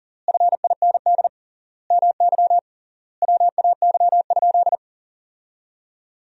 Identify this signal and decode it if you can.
Morse code, heard as a keyed tone, about 37 wpm, 700 Hz. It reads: FIND MY WAY?
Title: 3x Word-Spacing - Sentences from Top 200 Words - 40wpm